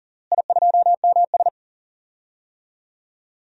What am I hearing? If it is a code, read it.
Morse code, heard as a keyed tone, about 40 wpm, 700 Hz. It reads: I2MS
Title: DXCC-100 Call Signs - 4 Characters - Repeated 2x - 40wpm